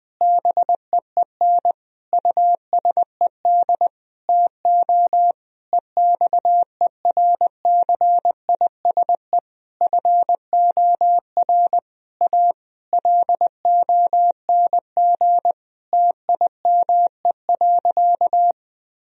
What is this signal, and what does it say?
Morse code, heard as a keyed tone, about 20 wpm, 700 Hz. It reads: BEEN USED TO EXERCISE FOR A LONG TIME.
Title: Sentences from Top 1000 Words - Review Part 4 - 20wpm